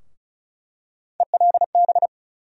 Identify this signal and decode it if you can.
Morse code, heard as a keyed tone, about 36 wpm, 700 Hz. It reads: ELB
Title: USA Call Sign Postfixes - 3 Letters 35wpm